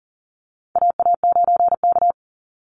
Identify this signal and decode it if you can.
Morse code, heard as a keyed tone, about 40 wpm, 700 Hz. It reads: AA9K
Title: Call Signs Repeated 3x - 40wpm